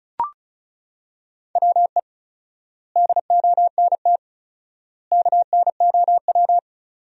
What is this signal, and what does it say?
Morse code, heard as a keyed tone, about 34 wpm, 700 Hz. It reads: WE DONT KNOW
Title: 4x Word-Spacing - Sentences from Top 400 Words - 35wpm